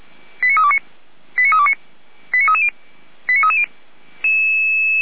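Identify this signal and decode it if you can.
Five-tone selective call (ZVEI), heard as quick runs of five tones, tones of about 70 ms. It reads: 89329 89329 89330 89330
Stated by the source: ZVEI five-tone selective call received on an SDR.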